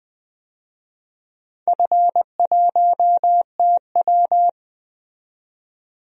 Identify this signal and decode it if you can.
Morse code, heard as a keyed tone, about 20 wpm, 700 Hz. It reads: F1TW
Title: DXCC-100 Call Signs - 4 Characters - 20wpm